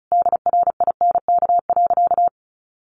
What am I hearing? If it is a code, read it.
Morse code, heard as a keyed tone, about 34 wpm, 700 Hz. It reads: DRINK.
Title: Sentences from Top 1500 Words 35wpm